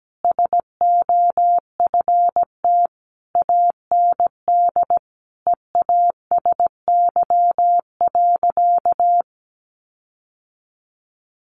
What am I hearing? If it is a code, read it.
Morse code, heard as a keyed tone, about 17 wpm, 700 Hz. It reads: SOFT AND EASY.
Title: Sentences from Top 1500 Words 17wpm